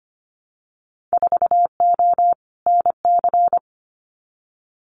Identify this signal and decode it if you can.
Morse code, heard as a keyed tone, about 25 wpm, 700 Hz. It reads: 4O NC
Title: Winter Field Day Exchanges 25wpm